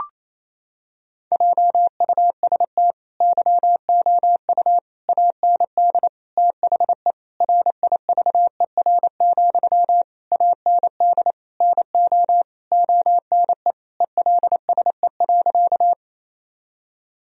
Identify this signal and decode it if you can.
Morse code, heard as a keyed tone, about 28 wpm, 700 Hz. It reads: JUST YOU AND THE RIVER, AND NO ONE ELSE.
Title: Sentences from Top 1000 Words - Review Part 1 - 28wpm